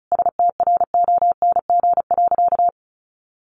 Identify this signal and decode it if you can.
Morse code, heard as a keyed tone, about 34 wpm, 700 Hz. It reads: STRONG.
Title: Sentences from Top 1400 Words 35wpm